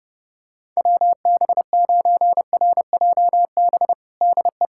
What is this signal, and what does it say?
Morse code, heard as a keyed tone, about 32 wpm, 700 Hz. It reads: WB9RJB DE
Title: USA Callsign DE Callsign 30wpm